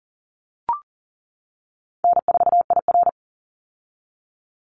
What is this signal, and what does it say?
Morse code, heard as a keyed tone, about 38 wpm, 700 Hz. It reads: N4IR
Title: Easy Call Signs - 40wpm